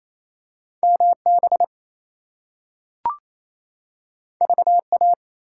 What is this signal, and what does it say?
Morse code, heard as a keyed tone, about 28 wpm, 700 Hz. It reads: MB VA
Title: US States and Canadian Provinces 28wpm